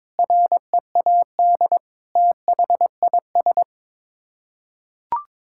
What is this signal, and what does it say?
Morse code, heard as a keyed tone, about 22 wpm, 700 Hz. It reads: READ THIS
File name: Sets of 9 Words - 22wpm